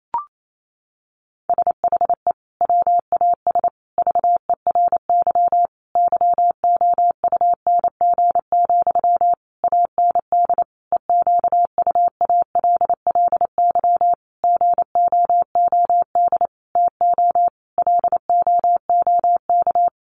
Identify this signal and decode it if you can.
Morse code, heard as a keyed tone, about 28 wpm, 700 Hz. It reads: SHE WAS VERY YOUNG, AND EQUALLY GOOD TO LOOK
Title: Sentences from Top 1400 Words 28wpm